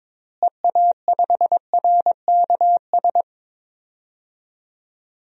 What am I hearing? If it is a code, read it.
Morse code, heard as a keyed tone, about 22 wpm, 700 Hz. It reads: EA5RKS
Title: DXCC-100 Call Signs - 6 Characters - Repeated 2x - 22wpm